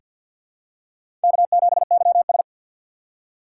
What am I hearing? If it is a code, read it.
Morse code, heard as a keyed tone, about 47 wpm, 700 Hz. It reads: K7YS
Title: USA Call Signs - 2x Character Spacing - 50wpm